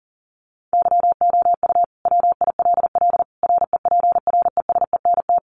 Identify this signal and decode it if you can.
Morse code, heard as a keyed tone, about 40 wpm, 700 Hz. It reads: YOU WILL REPRESENT
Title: Sentences from Top 1200 Words 40wpm